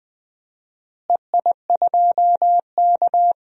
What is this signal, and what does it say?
Morse code, heard as a keyed tone, about 20 wpm, 700 Hz. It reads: EI2K
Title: DXCC-100 Call Signs - 4 Characters - Repeated 3x - 20wpm